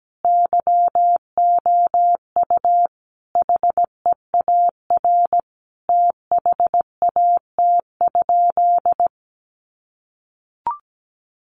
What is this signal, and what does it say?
Morse code, heard as a keyed tone, about 17 wpm, 700 Hz. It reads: YOU HEAR THAT?
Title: Sentences from Top 1500 Words 17wpm